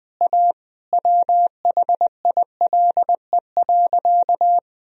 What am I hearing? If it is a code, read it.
Morse code, heard as a keyed tone, about 19 wpm, 700 Hz. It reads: A WHILE.
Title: Sentences from Top 1000 Words - Review Part 1 - 20wpm